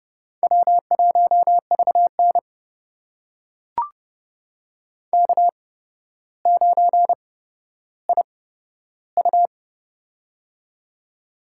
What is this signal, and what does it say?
Morse code, heard as a keyed tone, about 30 wpm, spaced as for 13 wpm, 700 Hz. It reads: W1VN K9IU
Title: USA Call Signs - 4 Characters - 8x Character Spacing - 30wpm